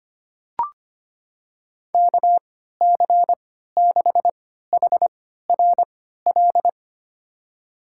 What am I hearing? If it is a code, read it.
Morse code, heard as a keyed tone, about 25 wpm, spaced as for 14 wpm, 700 Hz. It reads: KC6HRL
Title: USA Call Signs - 3x Character Spacing - 25wpm